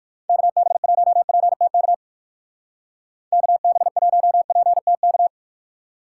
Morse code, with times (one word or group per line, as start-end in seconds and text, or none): 0.3-1.9 KB1PTK
3.3-5.3 KB1PTK